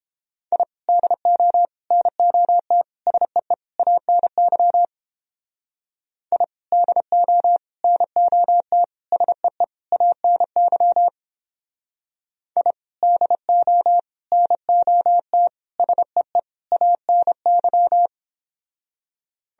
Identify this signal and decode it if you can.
Morse code, heard as a keyed tone, about 31 wpm, 700 Hz. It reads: I DO NOT SEE ANY I DO NOT SEE ANY I DO NOT SEE ANY